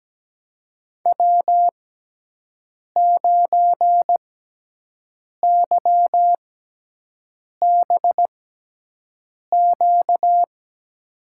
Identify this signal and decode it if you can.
Morse code, heard as a keyed tone, about 17 wpm, spaced as for 6 wpm, 700 Hz. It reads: W9YBQ
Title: USA Call Signs - 6x Character Spacing - 17wpm